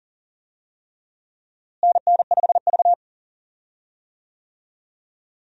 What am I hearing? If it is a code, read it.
Morse code, heard as a keyed tone, about 40 wpm, 700 Hz. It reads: NN5V
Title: USA Call Signs - 4 Characters - Repeated 3x - 40wpm